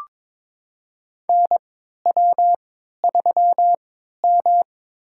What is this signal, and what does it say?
Morse code, heard as a keyed tone, about 22 wpm, spaced as for 12 wpm, 700 Hz. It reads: NW3M
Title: USA Call Signs - 3x Character Spacing - 22wpm